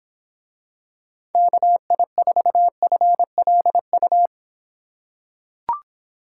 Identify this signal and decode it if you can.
Morse code, heard as a keyed tone, about 26 wpm, 700 Hz. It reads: KI4FLU